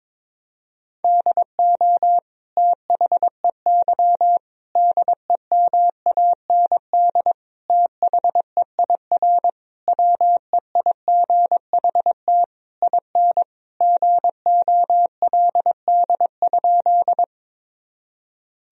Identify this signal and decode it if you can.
Morse code, heard as a keyed tone, about 21 wpm, 700 Hz. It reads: DO THEY DEMAND THEIR WEIGHT IN GOLD?
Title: Sentences from Top 1000 Words - Review Part 3 - 22wpm